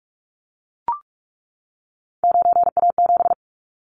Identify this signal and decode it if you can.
Morse code, heard as a keyed tone, about 44 wpm, 700 Hz. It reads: 9A7
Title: Callsign Prefixes for Top 100 DXCC - 45wpm